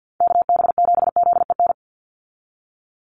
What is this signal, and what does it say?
Morse code, heard as a keyed tone, about 50 wpm, 700 Hz. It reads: KB7ZEN
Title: USA Call Signs 50wpm